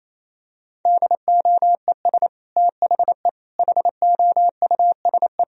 Morse code, heard as a keyed tone, about 28 wpm, 700 Hz. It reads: DOES THE HOUSE
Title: Sentences from Top 1000 Words - Review Part 2 - 28wpm